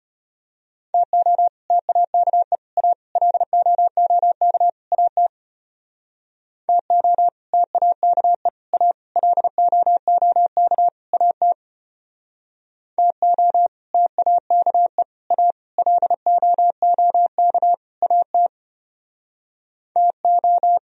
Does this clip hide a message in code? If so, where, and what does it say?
Yes: Morse code from 0.9 to 20.8 s, heard as a keyed tone, about 32 wpm, 700 Hz. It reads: TO TAKE A LOOK AT TO TAKE A LOOK AT TO TAKE A LOOK AT TO